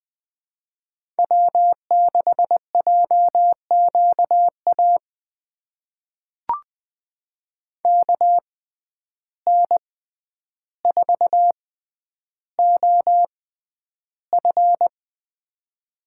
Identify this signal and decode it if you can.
Morse code, heard as a keyed tone, about 20 wpm, spaced as for 10 wpm, 700 Hz. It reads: W6JQA KN4OF
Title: USA Call Signs - 6x Character Spacing - 20wpm